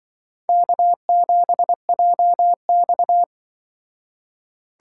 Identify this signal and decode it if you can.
Morse code, heard as a keyed tone, about 25 wpm, 700 Hz. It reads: K7JX
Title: Speed Racing - USA 1x2 Call Signs 20wpm